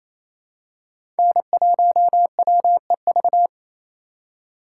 Morse code, heard as a keyed tone, about 28 wpm, 700 Hz. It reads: N1WEV
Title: USA Call Signs - 5 Characters - 4x Character Spacing - 28wpm